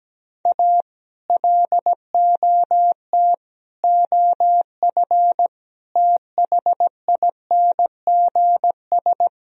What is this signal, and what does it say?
Morse code, heard as a keyed tone, about 17 wpm, 700 Hz. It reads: A LOT OF THINGS
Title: Speed-Racing - Sets of 4 Words 17wpm